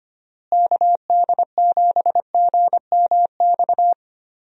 Morse code, heard as a keyed tone, about 25 wpm, 700 Hz. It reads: KD7GMX
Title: USA Call Signs - 6 Characters - Repeated 3x - 25wpm